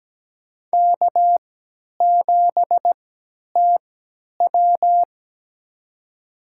Morse code, heard as a keyed tone, about 17 wpm, spaced as for 10 wpm, 700 Hz. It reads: K7TW
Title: USA Call Signs - 4 Characters - 3x Character Spacing - 17wpm